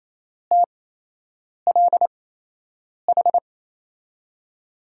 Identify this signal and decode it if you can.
Morse code, heard as a keyed tone, about 28 wpm, spaced as for 8 wpm, 700 Hz. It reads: TLH
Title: USA Call Signs - 6 Characters - 8x Character Spacing - 28wpm